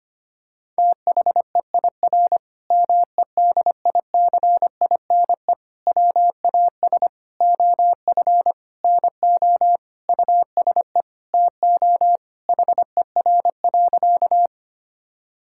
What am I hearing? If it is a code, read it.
Morse code, heard as a keyed tone, about 25 wpm, 700 Hz. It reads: THEIR MEDICINE WAS OF NO USE TO HER.
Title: Sentences from Top 1000 Words - Review Part 6 - 25wpm